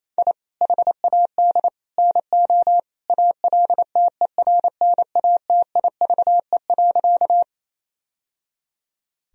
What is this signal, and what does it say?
Morse code, heard as a keyed tone, about 28 wpm, 700 Hz. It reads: I HAD NO ALTERNATIVE.
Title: Sentences from Top 1000 Words - Review Part 4 - 28wpm